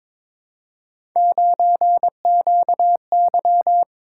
Morse code, heard as a keyed tone, about 22 wpm, 700 Hz. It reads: 9QY
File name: practice / 22wpm / Three Letter-Number Combinations - 22wpm